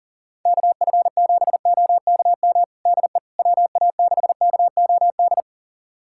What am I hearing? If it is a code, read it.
Morse code, heard as a keyed tone, about 40 wpm, 700 Hz. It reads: KF7OKM DE WA6KOD
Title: USA Callsign DE Callsign - 6 Characters - 40wpm